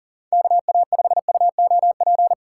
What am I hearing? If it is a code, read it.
Morse code, heard as a keyed tone, about 40 wpm, 700 Hz. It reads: KA5UOP
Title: USA Call Signs - 6 Characters - 3x Character Spacing - 40wpm